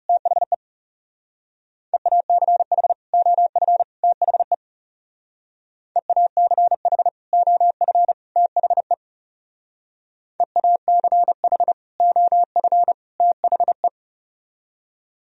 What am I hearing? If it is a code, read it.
Morse code, heard as a keyed tone, about 36 wpm, 700 Hz. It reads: THE EACH OF THE EACH OF THE EACH OF THE
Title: Speed-Racing - Sets of 3 Words 30wpm